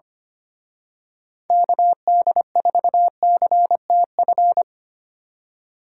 Morse code, heard as a keyed tone, about 28 wpm, 700 Hz. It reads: KD4CTF
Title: Speed Racing - USA 2x3 Call Signs 25wpm